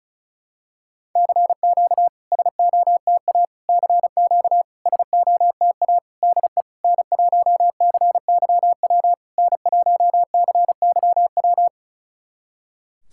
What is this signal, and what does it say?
Morse code, heard as a keyed tone, about 36 wpm, 690 Hz. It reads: CQ SOTA CQ SOTA DE N1CYW N1CYW
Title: SOTA QSO Practice 35wpm